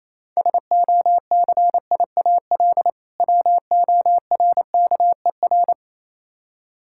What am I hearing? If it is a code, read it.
Morse code, heard as a keyed tone, about 28 wpm, 700 Hz. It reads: SOCIAL WORKER
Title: Practice QSO - Exchange 3 - More Info - 28wpm